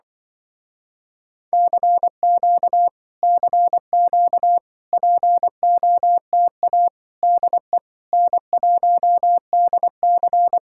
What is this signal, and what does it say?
Morse code, heard as a keyed tone, about 24 wpm, 700 Hz. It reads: CQ CQ POTA DE N1DC
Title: Speed-Racing Common QSO Phrases - 20wpm